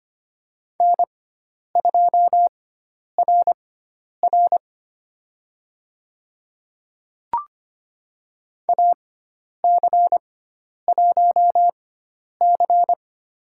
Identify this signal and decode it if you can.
Morse code, heard as a keyed tone, about 25 wpm, spaced as for 10 wpm, 700 Hz. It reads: N2RR AC1C
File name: Lesson 16g - Letter c - Call Signs 25wpm